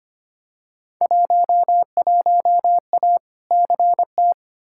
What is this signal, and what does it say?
Morse code, heard as a keyed tone, about 25 wpm, 700 Hz. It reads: 11A CT
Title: Field Day Exchanges Encore - 25wpm